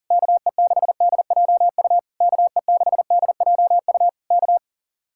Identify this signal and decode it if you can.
Morse code, heard as a keyed tone, about 37 wpm, 700 Hz. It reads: KE6DJU KE6DJU K
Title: Speed-Racing Common QSO Phrases - 30wpm